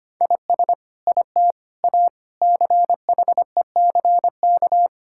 Morse code, heard as a keyed tone, about 25 wpm, 700 Hz. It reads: IS IT A CHECK
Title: Sentences from Top 1000 Words - Review Part 6 - 25wpm